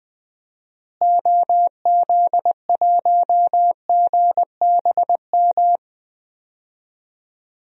Morse code, heard as a keyed tone, about 21 wpm, 700 Hz. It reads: OZ1GBM
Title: DXCC-100 Call Signs - 6 Characters - Repeated 3x - 20wpm